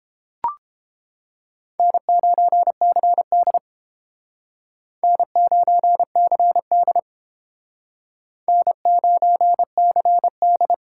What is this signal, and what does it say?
Morse code, heard as a keyed tone, about 30 wpm, 700 Hz. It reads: N9CD N9CD N9CD